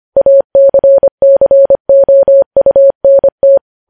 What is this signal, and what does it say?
Morse code, heard as a keyed tone, about 25 wpm, 550 Hz. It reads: ACCOUNT
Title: News Headlines at 25